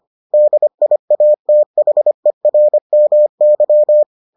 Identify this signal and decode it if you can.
Morse code, heard as a keyed tone, about 25 wpm, 600 Hz. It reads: DIATHERMY